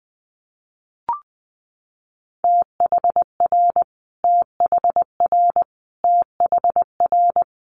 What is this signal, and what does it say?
Morse code, heard as a keyed tone, about 20 wpm, 700 Hz. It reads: THR THR THR
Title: Multi-letter Phonemes Repeated 3x 20wpm